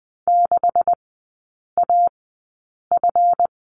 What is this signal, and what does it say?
Morse code, heard as a keyed tone, about 20 wpm, spaced as for 8 wpm, 700 Hz. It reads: 6AF
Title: 138 Lesson 23e - Letter f - Three Characters 20wpm